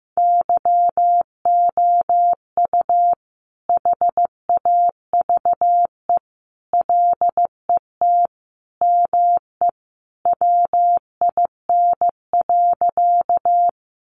Morse code, heard as a keyed tone, about 15 wpm, 700 Hz. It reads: YOU HAVE LET ME WIN.
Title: Sentences from Top 1500 Words 15wpm